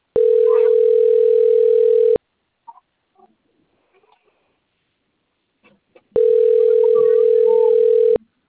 Ringback tone (North America), 440 + 480 Hz.